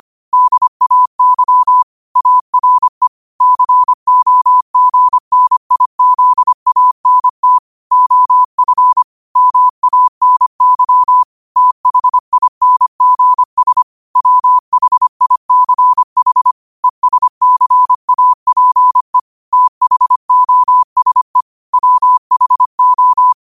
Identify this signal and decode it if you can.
Morse code, heard as a keyed tone, about 25 wpm, 1000 Hz. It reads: DAY ARE COGNIZANT OF MANY THINGS WHICH ESCAPE THOSE WHO
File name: qotdfiles / 25wpm